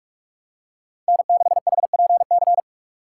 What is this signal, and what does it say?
Morse code, heard as a keyed tone, about 46 wpm, 700 Hz. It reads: N6HPC
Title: USA Call Signs - 5 Characters - 3x Character Spacing - 45wpm